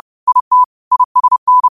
Morse code, heard as a keyed tone, about 30 wpm, 1000 Hz. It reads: IT ISN
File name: qotdfiles / 30wpm